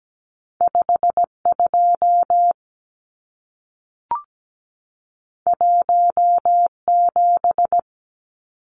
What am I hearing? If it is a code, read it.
Morse code, heard as a keyed tone, about 17 wpm, 700 Hz. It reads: 52 17
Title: Numbers - 2-Digit 17wpm